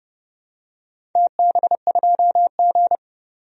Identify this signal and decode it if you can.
Morse code, heard as a keyed tone, about 29 wpm, 700 Hz. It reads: TB2G
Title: DXCC-100 Call Signs - 4 Characters - 30wpm